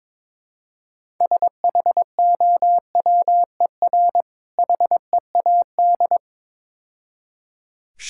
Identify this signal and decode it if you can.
Morse code, heard as a keyed tone, about 22 wpm, 700 Hz. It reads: SHOWER HEAD
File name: Easy Words - Speed-Racing with Hints - 22wpm